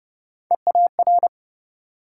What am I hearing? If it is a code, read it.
Morse code, heard as a keyed tone, about 40 wpm, 700 Hz. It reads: EAR